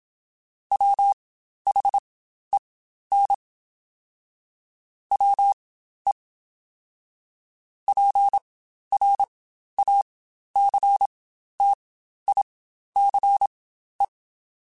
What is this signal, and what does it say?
Morse code, heard as a keyed tone, about 27 wpm, spaced as for 12 wpm, 800 Hz. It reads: WHEN WE PRACTICE
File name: Quotes Spaced / Quotes Spaced-mp3s / Deceive sp @25